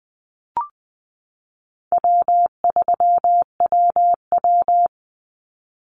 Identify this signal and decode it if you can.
Morse code, heard as a keyed tone, about 20 wpm, 700 Hz. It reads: W3WW
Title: Call Signs Repeated 3x - 20wpm